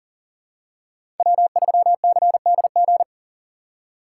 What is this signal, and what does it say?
Morse code, heard as a keyed tone, about 40 wpm, 700 Hz. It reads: W3CDG
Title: USA Call Signs - 5 Characters - 4x Character Spacing - 40wpm